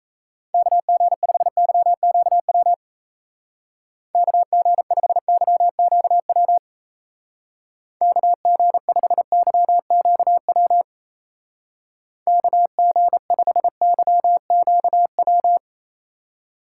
Morse code, heard as a keyed tone, about 34 wpm, 700 Hz. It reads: KG5YQW KG5YQW KG5YQW KG5YQW
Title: Speed Racing - USA 2x3 Call Signs 28wpm